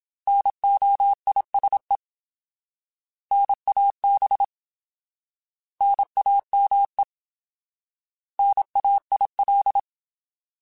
Morse code, heard as a keyed tone, about 27 wpm, 800 Hz. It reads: NOISE NAB NAME NAIL
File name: Alphabetical / Alphabetical mp3s / Ns @25